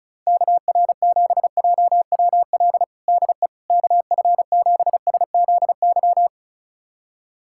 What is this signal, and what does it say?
Morse code, heard as a keyed tone, about 34 wpm, 700 Hz. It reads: KR7JWL DE KF7SZY
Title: USA Callsign DE Callsign - 6 Characters - 35wpm